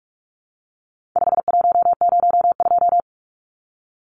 Morse code, heard as a keyed tone, about 44 wpm, 700 Hz. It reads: 5102
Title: Numbers - 4-Digit 45wpm